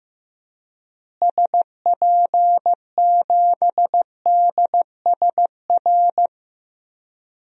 Morse code, heard as a keyed tone, about 15 wpm, 700 Hz. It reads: SP7DSR
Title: DXCC-100 Call Signs - 6 Characters - Repeated 3x - 15wpm